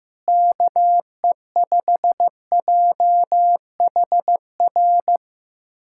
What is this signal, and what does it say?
Morse code, heard as a keyed tone, about 15 wpm, 700 Hz. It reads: KE5JHR
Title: USA Callsign DE Callsign - 6 Characters - 15wpm